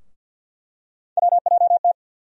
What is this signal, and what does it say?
Morse code, heard as a keyed tone, about 49 wpm, 700 Hz. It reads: WJT
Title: USA Call Sign Postfixes - 3 Letters 50wpm